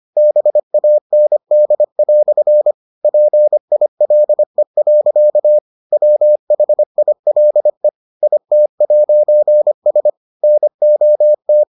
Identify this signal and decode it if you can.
Morse code, heard as a keyed tone, about 25 wpm, 600 Hz. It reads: BAND" PILE. WHILE IT'S NOT